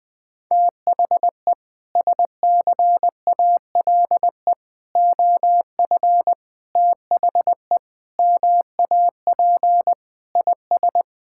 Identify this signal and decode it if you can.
Morse code, heard as a keyed tone, about 20 wpm, 700 Hz. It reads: THE SCALE OF THE MAP IS
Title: Sentences from Top 1000 Words - Review Part 5 - 20wpm